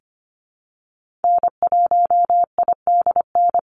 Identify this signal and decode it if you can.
Morse code, heard as a keyed tone, about 25 wpm, 700 Hz. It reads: N1IDN
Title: NA SOTA Activator Callsigns 25wpm